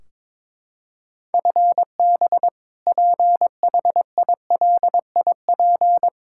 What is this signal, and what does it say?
Morse code, heard as a keyed tone, about 22 wpm, 700 Hz. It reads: FB PHILIP
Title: Practice QSO - Exchange 3 - More Info - 22wpm